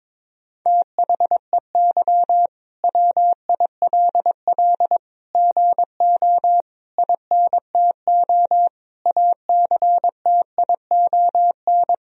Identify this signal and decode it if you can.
Morse code, heard as a keyed tone, about 22 wpm, 700 Hz. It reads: THEY WILL GO INTO ACTION
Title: Sentences from Top 1000 Words - Review Part 4 - 22wpm